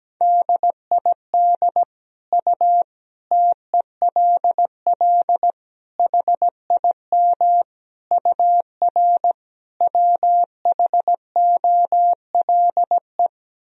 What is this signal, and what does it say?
Morse code, heard as a keyed tone, about 17 wpm, 700 Hz. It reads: DID U TELL HIM UR WHOLE
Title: Sentences from Top 400 Words and Abbreviations 17wpm